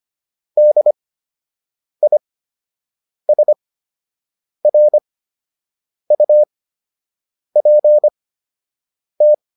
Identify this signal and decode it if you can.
Morse code, heard as a keyed tone, about 25 wpm, spaced as for 7 wpm, 600 Hz. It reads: DISRUPT
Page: fundamental-practice-files